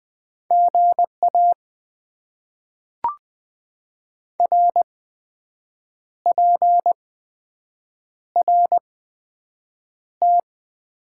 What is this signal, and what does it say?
Morse code, heard as a keyed tone, about 20 wpm, spaced as for 7 wpm, 700 Hz. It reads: GA RPRT
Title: Top 90 QSO Elements - 8x Character Spacing - 20wpm